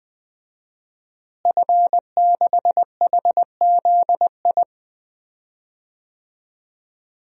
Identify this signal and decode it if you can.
Morse code, heard as a keyed tone, about 20 wpm, 700 Hz. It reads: F6HZI
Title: DXCC-100 Call Signs - 5 Characters - Repeated 3x - 20wpm